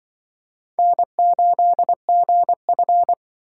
Morse code, heard as a keyed tone, about 24 wpm, 700 Hz. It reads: N8GF
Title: Speed Racing - USA 1x2 Call Signs 20wpm